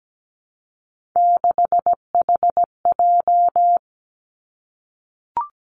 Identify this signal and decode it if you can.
Morse code, heard as a keyed tone, about 17 wpm, 700 Hz. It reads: 6HJ
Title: Three Challenging Letter-Number Combinations 17wpm